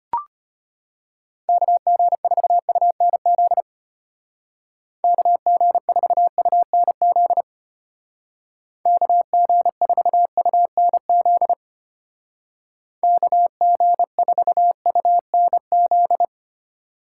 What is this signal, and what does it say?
Morse code, heard as a keyed tone, about 33 wpm, 700 Hz. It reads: KG4UNZ KG4UNZ KG4UNZ KG4UNZ